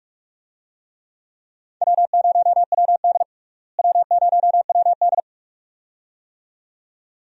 Morse code, heard as a keyed tone, about 44 wpm, 700 Hz. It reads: W0WD W0WD
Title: USA Call Signs - 4 Characters - Repeated 2x - 45wpm